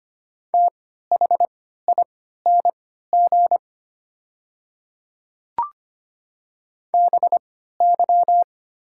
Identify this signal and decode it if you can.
Morse code, heard as a keyed tone, about 25 wpm, spaced as for 14 wpm, 700 Hz. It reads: THING BY
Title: 3x Character Spacing - Top 100 Words - 25wpm